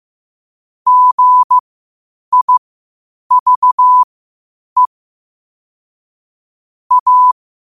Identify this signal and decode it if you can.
Morse code, heard as a keyed tone, about 15 wpm, spaced as for 8 wpm, 1000 Hz. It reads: GIVE A